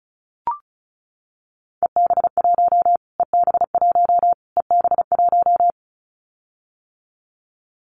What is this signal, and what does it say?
Morse code, heard as a keyed tone, about 34 wpm, 700 Hz. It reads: EB1 EB1 EB1
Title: Callsign Prefixes for Top 25 DXCC - Repeated 3x - 35wpm